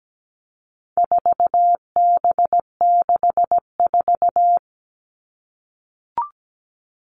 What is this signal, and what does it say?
Morse code, heard as a keyed tone, about 17 wpm, 700 Hz. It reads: 4B64
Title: All Cut Numbers - 4-Digit 17wpm